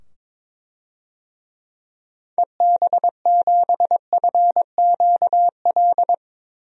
Morse code, heard as a keyed tone, about 22 wpm, 700 Hz. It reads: EB7FQL
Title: DXCC-100 Call Signs - 6 Characters - 22wpm